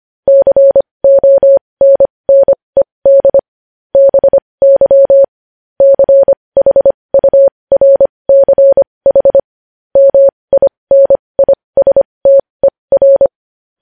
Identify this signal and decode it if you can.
Morse code, heard as a keyed tone, about 25 wpm, spaced as for 20 wpm, 550 Hz. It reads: CONNED BY CHURCH MINISTER
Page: news-headlines-in-morse-code-at-20-wpm